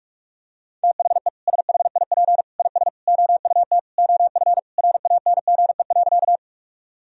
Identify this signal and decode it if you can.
Morse code, heard as a keyed tone, about 46 wpm, 700 Hz. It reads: THE SHIP IS OUT OF RANGE.
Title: Sentences from Top 1000 Words - Review Part 4 - 45wpm